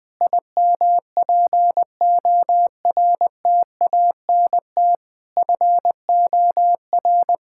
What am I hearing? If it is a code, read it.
Morse code, heard as a keyed tone, about 20 wpm, 700 Hz. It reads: IMPORTANT FOR
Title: Sentences from Top 1000 Words - Review Part 6 - 20wpm